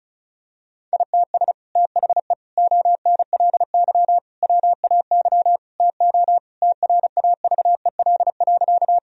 Morse code, heard as a keyed tone, about 34 wpm, 700 Hz. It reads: ITS THE ONLY WAY TO TRAVEL.
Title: Sentences from Top 1000 Words - Review Part 2 - 35wpm